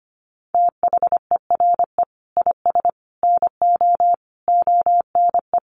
Morse code, heard as a keyed tone, about 25 wpm, 700 Hz. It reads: THERE IS NO ONE
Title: Sentences from Top 1000 Words 25wpm